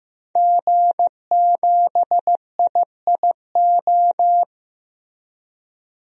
Morse code, heard as a keyed tone, about 15 wpm, 700 Hz. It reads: G7IIO
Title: DXCC-100 Call Signs - 5 Characters - 15wpm